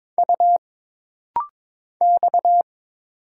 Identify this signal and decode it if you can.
Morse code, heard as a keyed tone, about 22 wpm, 700 Hz. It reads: U X
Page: This rapid-fire timing is geared to push your ICR (Instant Character Recognition) skills.